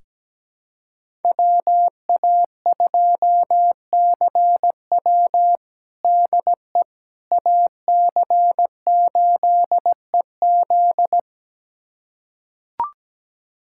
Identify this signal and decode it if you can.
Morse code, heard as a keyed tone, about 17 wpm, 700 Hz. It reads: WA2CW DE AC8EZ
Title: USA Callsign DE Callsign - 5 Characters - 17wpm